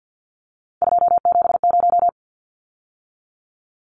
Morse code, heard as a keyed tone, about 50 wpm, 700 Hz. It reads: 270
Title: Numbers - 3-Digit 50wpm